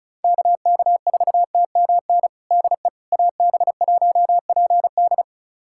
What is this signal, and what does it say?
Morse code, heard as a keyed tone, about 36 wpm, 700 Hz. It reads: KK4TMN DE AB1PD
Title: USA Callsign DE Callsign 35wpm